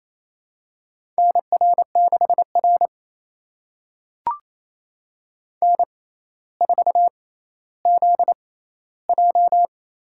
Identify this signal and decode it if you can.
Morse code, heard as a keyed tone, about 28 wpm, spaced as for 14 wpm, 700 Hz. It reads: NR6R N4ZJ
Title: USA Call Signs - 4 Characters - 6x Character Spacing - 28wpm